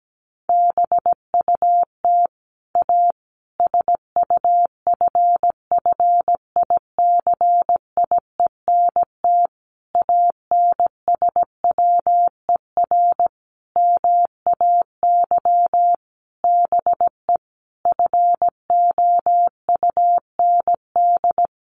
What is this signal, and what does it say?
Morse code, heard as a keyed tone, about 17 wpm, 700 Hz. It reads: BUT A SUFFICIENT ANSWER MAY BE FOUND
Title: Sentences from Top 1400 Words 17wpm